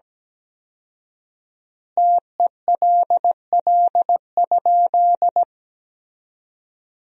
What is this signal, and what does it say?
Morse code, heard as a keyed tone, about 17 wpm, 700 Hz. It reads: TELL?
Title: 4x Word-Spacing - Sentences from Top 300 Words - 17wpm